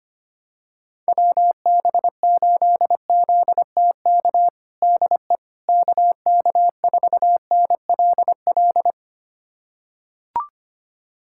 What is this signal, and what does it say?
Morse code, heard as a keyed tone, about 25 wpm, 700 Hz. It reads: WB8ZTK DE KK4NLL
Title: USA Callsign DE Callsign - 6 Characters - 25wpm